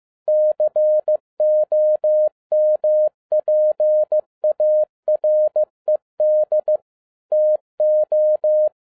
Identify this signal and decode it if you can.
Morse code, heard as a keyed tone, about 15 wpm, 610 Hz. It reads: COMPARED TO